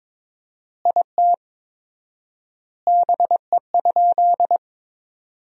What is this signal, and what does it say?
Morse code, heard as a keyed tone, about 22 wpm, 700 Hz. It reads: IT BE?
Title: 4x Word-Spacing - Sentences from Top 100 Words - 22wpm